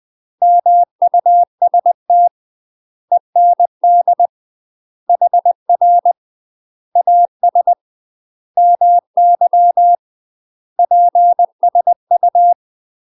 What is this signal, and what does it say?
Morse code, heard as a keyed tone, about 20 wpm, 700 Hz. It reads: MUST END HR AS MY PSU
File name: UK MASTER QSOs TRAINING / 20WPM / UK MASTER QSOs 0017